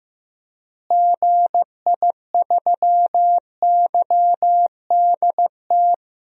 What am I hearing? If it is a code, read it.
Morse code, heard as a keyed tone, about 15 wpm, 700 Hz. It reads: GI3YDT
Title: DXCC-100 Call Signs - 6 Characters - Repeated 2x - 15wpm